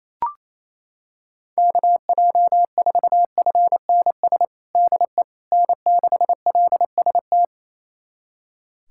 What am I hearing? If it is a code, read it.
Morse code, heard as a keyed tone, about 27 wpm, 700 Hz. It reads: KJ4FNS DE N6LST
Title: USA Callsign DE Callsign 28wpm